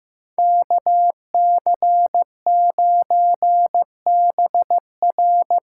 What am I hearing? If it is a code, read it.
Morse code, heard as a keyed tone, about 15 wpm, 700 Hz. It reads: KC9BR
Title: USA Call Signs - 6 Characters - 6x Character Spacing - 15wpm